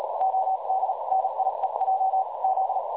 Morse code, heard as a keyed tone, about 38 wpm, spaced as for 33 wpm, 760 Hz, with noise. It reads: OD NIEJ Z
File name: dlazaawansowanych / nagraniazpasma / websdr recording 3565.0kHz